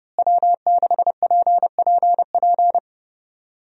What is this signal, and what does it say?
Morse code, heard as a keyed tone, about 30 wpm, 700 Hz. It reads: W6PPP
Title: USA Call Signs - 5 Characters - Repeated 2x - 30wpm